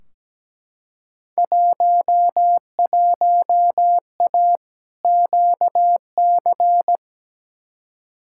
Morse code, heard as a keyed tone, about 17 wpm, 700 Hz. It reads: 11A QC
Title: Field Day Exchanges Encore - 17wpm